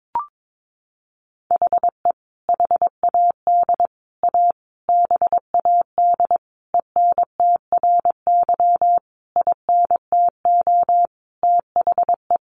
Morse code, heard as a keyed tone, about 22 wpm, 700 Hz. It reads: HE HAD A BAD ENTRY INTO THE
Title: Sentences from Top 1500 Words 22wpm